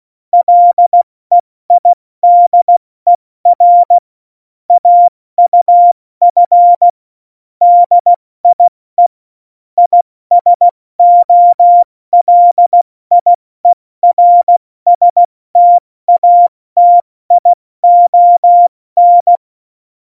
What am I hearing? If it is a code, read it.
Morse code, heard as a keyed tone, about 16 wpm, spaced as for 14 wpm, 700 Hz. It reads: LEIDER AUF DIE ISOLIERSTATION
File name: Morsekurs / Witze / Witz 02. Januar 2018, 70 F80